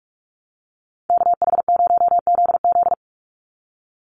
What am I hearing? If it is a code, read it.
Morse code, heard as a keyed tone, about 44 wpm, 700 Hz. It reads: KH0ZZ
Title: Call Signs Repeated 2x - 45wpm